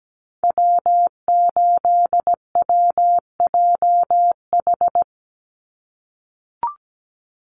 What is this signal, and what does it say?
Morse code, heard as a keyed tone, about 17 wpm, 700 Hz. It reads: W8WJH
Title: USA Call Signs 17wpm